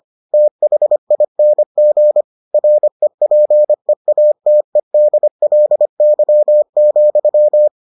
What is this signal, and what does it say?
Morse code, heard as a keyed tone, about 25 wpm, 600 Hz. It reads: THING REPEATEDLY,